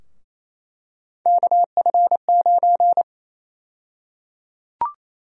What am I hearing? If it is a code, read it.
Morse code, heard as a keyed tone, about 28 wpm, 700 Hz. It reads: KF9
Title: USA Call Sign Prefixes - 28wpm